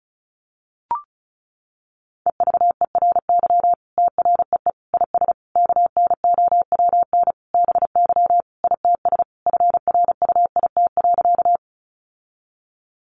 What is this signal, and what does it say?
Morse code, heard as a keyed tone, about 34 wpm, 700 Hz. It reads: EVERY TREE IS KNOWN BY ITS FRUIT.
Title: Sentences from Top 1300 Words 35wpm